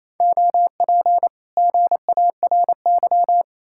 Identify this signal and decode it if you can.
Morse code, heard as a keyed tone, about 37 wpm, 700 Hz. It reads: OP GARY